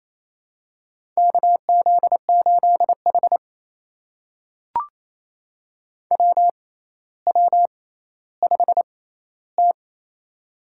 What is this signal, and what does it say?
Morse code, heard as a keyed tone, about 28 wpm, spaced as for 14 wpm, 700 Hz. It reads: KZ8H WW5T
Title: USA Call Signs - 4 Characters - 6x Character Spacing - 28wpm